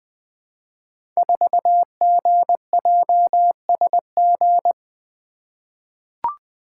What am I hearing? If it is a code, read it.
Morse code, heard as a keyed tone, about 20 wpm, 700 Hz. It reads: 4GJSG
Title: Code Groups - 5 Letter:Numbers - 20wpm